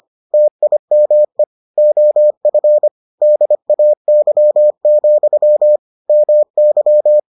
Morse code, heard as a keyed tone, about 25 wpm, 600 Hz. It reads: TIME OF DAY, MY